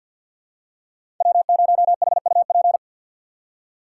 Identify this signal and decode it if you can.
Morse code, heard as a keyed tone, about 53 wpm, 690 Hz. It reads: W0HUP
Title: USA Call Signs - 5 Characters - 6x Character Spacing - 50wpm